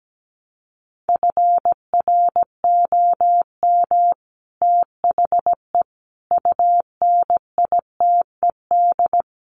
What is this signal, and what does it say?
Morse code, heard as a keyed tone, about 17 wpm, 700 Hz. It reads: FROM THE UNITED
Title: Sets of 4 Words 17wpm